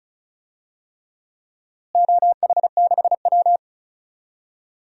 Morse code, heard as a keyed tone, about 36 wpm, 700 Hz. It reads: OH6W
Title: DXCC-100 Call Signs - 4 Characters - Repeated 3x - 35wpm